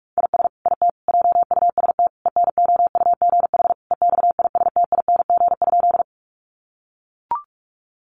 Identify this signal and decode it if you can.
Morse code, heard as a keyed tone, about 44 wpm, 700 Hz. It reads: IS IT JUST ENOUGH EXISTING?
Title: Sentences from Top 1500 Words 45wpm